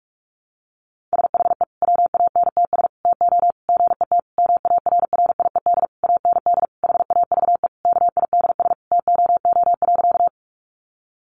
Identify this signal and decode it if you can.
Morse code, heard as a keyed tone, about 44 wpm, 700 Hz. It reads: SHE WANTS TO GET MARRIED AND HAVE KIDS TOO.
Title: Easy Phrases 45wpm